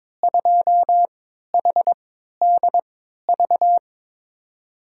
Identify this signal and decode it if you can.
Morse code, heard as a keyed tone, about 22 wpm, spaced as for 12 wpm, 700 Hz. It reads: 2HDV
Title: USA Call Signs - 5 Characters - 3x Character Spacing - 22wpm